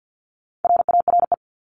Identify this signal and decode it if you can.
Morse code, heard as a keyed tone, about 50 wpm, 700 Hz. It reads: RARE
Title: LY Suffix - Top 100 Words with Base Words 50wpm